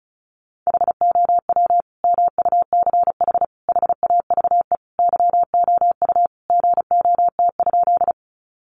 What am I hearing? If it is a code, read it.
Morse code, heard as a keyed tone, about 34 wpm, 700 Hz. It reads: HOW MUCH HAVE YOU GOT?
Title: Sentences from Top 300 Words 35wpm